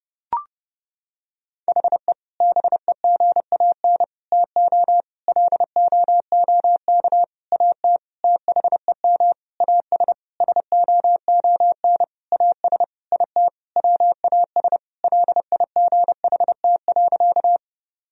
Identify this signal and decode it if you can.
Morse code, heard as a keyed tone, about 30 wpm, 700 Hz. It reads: HE BEGAN TO LOOK AT THEM AS SOON AS IT WAS LIGHT.
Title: Sentences from Top 1000 Words - Review Part 1 - 30wpm